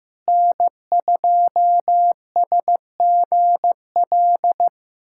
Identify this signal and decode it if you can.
Morse code, heard as a keyed tone, about 15 wpm, 700 Hz. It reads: N2SGL
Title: USA Callsign DE Callsign - 5 Characters - 15wpm